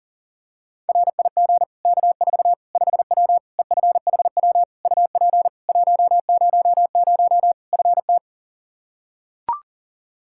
Morse code, heard as a keyed tone, about 38 wpm, 700 Hz. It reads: RIG K4 5W EFHW UP 100 FT
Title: Practice QSO - Exchange 2 - Info - 40wpm